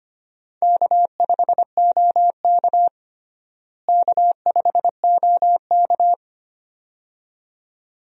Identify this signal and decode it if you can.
Morse code, heard as a keyed tone, about 25 wpm, 700 Hz. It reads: K5OK K5OK
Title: USA Call Signs - 4 Characters - Repeated 2x - 25wpm